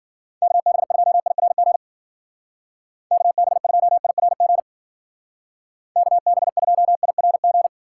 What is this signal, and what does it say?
Morse code, heard as a keyed tone, about 53 wpm, 700 Hz. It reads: KB2IRG KB2IRG KB2IRG